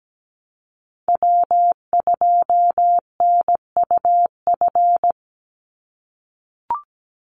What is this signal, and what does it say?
Morse code, heard as a keyed tone, about 17 wpm, 700 Hz. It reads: W2NUF
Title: USA Call Signs 17wpm